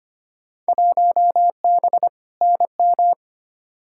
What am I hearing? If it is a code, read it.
Morse code, heard as a keyed tone, about 25 wpm, 700 Hz. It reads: 1B NM
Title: Field Day Exchanges Encore - 25wpm